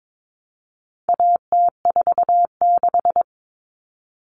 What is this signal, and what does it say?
Morse code, heard as a keyed tone, about 22 wpm, 700 Hz. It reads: AT46
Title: All Cut Numbers - 4-Digit 22wpm